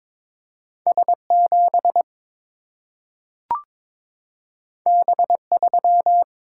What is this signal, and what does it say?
Morse code, heard as a keyed tone, about 22 wpm, 700 Hz. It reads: S7 B3
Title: Single-Letter Single-Number 22wpm